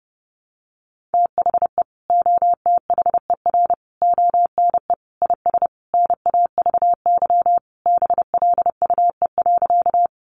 Morse code, heard as a keyed tone, about 30 wpm, 700 Hz. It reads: THE OTHER ONE IS NAVY BLUE.
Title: Additional Easy Phrases 30wpm